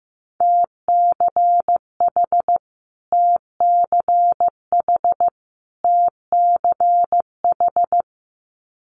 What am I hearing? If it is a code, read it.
Morse code, heard as a keyed tone, about 14 wpm, 700 Hz. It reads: TCH TCH TCH
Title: Multi-letter Phonemes Repeated 3x 15wpm